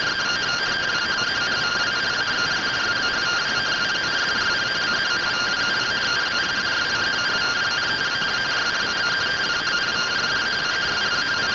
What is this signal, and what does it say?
Two-tone teleprinter signal, about 170 Hz shift.
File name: findings / CT AXIAL -111.gen-39 rtty rx